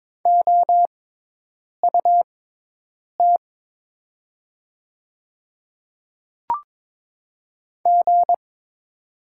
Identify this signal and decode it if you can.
Morse code, heard as a keyed tone, about 23 wpm, spaced as for 8 wpm, 700 Hz. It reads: OUT G
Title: 6x Character Spacing - Top 500 Words - 22wpm